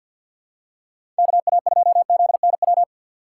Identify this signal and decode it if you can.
Morse code, heard as a keyed tone, about 46 wpm, 700 Hz. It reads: KA2ZNW
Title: USA Call Signs - 6 Characters - 4x Character Spacing - 50wpm